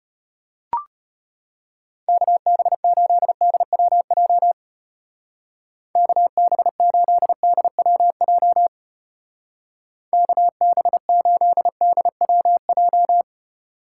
Morse code, heard as a keyed tone, about 31 wpm, 700 Hz. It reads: KB8DWJ KB8DWJ KB8DWJ
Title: Speed Racing - USA 2x3 Call Signs 25wpm